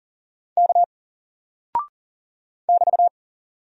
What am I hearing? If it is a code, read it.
Morse code, heard as a keyed tone, about 38 wpm, 700 Hz. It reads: K <BT>
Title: Single Character (Letter:Number:Punctuation:Prosign) Alt 40wpm